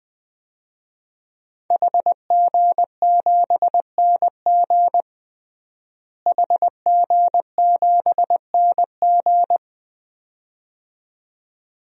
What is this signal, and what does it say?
Morse code, heard as a keyed tone, about 21 wpm, 700 Hz. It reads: HG7NG HG7NG
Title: DXCC-100 Call Signs - 5 Characters - Repeated 2x - 20wpm